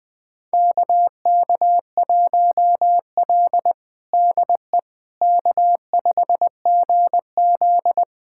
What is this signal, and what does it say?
Morse code, heard as a keyed tone, about 20 wpm, 700 Hz. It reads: KK1L DE K5GZ
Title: USA Callsign DE Callsign - 4 Characters - 20wpm